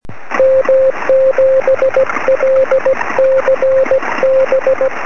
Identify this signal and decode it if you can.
Morse code, heard as a keyed tone, about 17 wpm, 530 Hz, with radio noise. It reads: M7LCB
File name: sm7lcb / 20 23cm / 1844 dk3wg 23cm